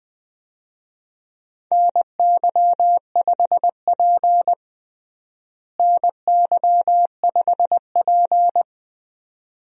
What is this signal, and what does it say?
Morse code, heard as a keyed tone, about 21 wpm, 700 Hz. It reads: NY5P NY5P
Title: USA Call Signs - 4 Characters - Repeated 2x - 20wpm